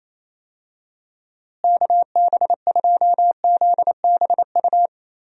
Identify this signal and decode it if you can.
Morse code, heard as a keyed tone, about 27 wpm, 700 Hz. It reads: KB2ZBU
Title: USA Call Signs - 6 Characters - Repeated 3x - 28wpm